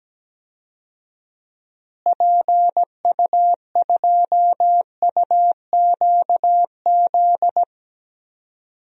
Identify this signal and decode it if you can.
Morse code, heard as a keyed tone, about 17 wpm, 700 Hz. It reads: PU2UQZ
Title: DXCC-100 Call Signs - 6 Characters - Repeated 3x - 17wpm